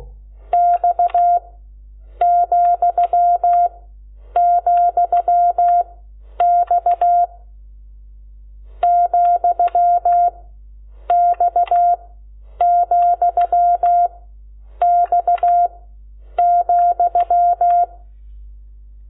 Morse code, heard as a keyed tone, about 16 wpm, 700 Hz. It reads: X , , X , X , X ,
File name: Lecciones / CLASE Nº12